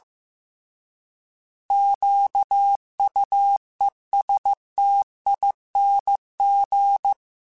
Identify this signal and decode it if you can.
Morse code, heard as a keyed tone, about 15 wpm, 800 Hz. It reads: QUESTING
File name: Alphabetical / Alphabetical mp3s / Qs @13